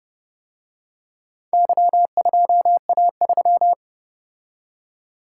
Morse code, heard as a keyed tone, about 29 wpm, 700 Hz. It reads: Y2A3